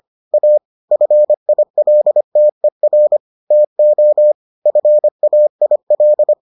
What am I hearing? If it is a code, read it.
Morse code, heard as a keyed tone, about 25 wpm, 600 Hz. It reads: A FILTER TO FAIL